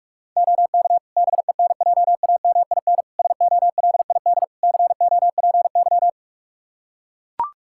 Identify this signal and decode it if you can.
Morse code, heard as a keyed tone, about 43 wpm, 700 Hz. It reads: OK BENJAMIN SOLID COPY
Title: K1USN SST QSO Practice 45wpm